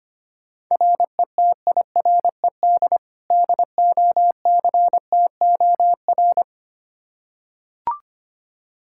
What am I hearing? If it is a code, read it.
Morse code, heard as a keyed tone, about 25 wpm, 700 Hz. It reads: RETIRED DOCTOR
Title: Practice QSO - Exchange 3 - More Info - 25wpm